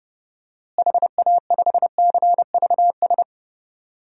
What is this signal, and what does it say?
Morse code, heard as a keyed tone, about 30 wpm, 700 Hz. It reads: HA5CVS
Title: DXCC-100 Call Signs - 6 Characters - Repeated 2x - 30wpm